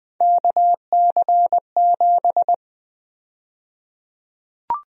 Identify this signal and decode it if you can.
Morse code, heard as a keyed tone, about 21 wpm, 700 Hz. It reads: KC7
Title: USA Call Sign Prefixes - 20wpm